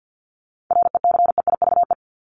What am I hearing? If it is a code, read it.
Morse code, heard as a keyed tone, about 50 wpm, 700 Hz. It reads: RECEIVE
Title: ED Suffix - Top 100 Words with Base Words 50wpm